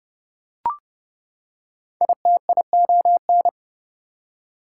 Morse code, heard as a keyed tone, about 30 wpm, 700 Hz. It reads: ITION
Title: Prefixes and Suffixes 30wpm